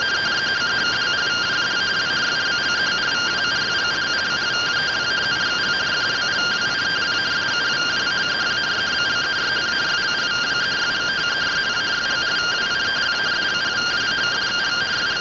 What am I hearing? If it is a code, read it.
Two-tone teleprinter signal, about 170 Hz shift.
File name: findings / AXIAL fr1 3mm.gen-45 rtty rx